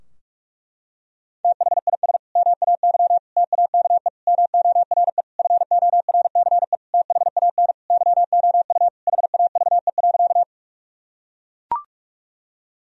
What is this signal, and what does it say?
Morse code, heard as a keyed tone, about 41 wpm, 700 Hz. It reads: THIS MAY TAKE MORE FORCE THAN YOU HAVE.
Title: Sentences from Top 1000 Words - Review Part 3 - 45wpm